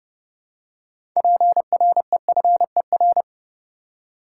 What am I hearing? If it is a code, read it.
Morse code, heard as a keyed tone, about 31 wpm, 700 Hz. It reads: PREFER
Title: Six Letter Words - 30wpm